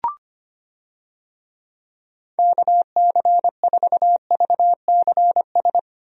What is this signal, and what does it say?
Morse code, heard as a keyed tone, about 25 wpm, 700 Hz. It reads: KC4VCS
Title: USA Call Signs - 6 Characters - Repeated 2x - 25wpm